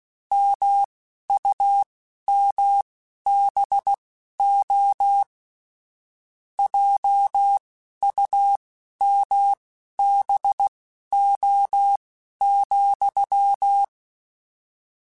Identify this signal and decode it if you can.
Morse code, heard as a keyed tone, about 16 wpm, spaced as for 11 wpm, 800 Hz. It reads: MUMBO JUMBO,
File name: Misc / Misc-mp3s / Word Pairs @10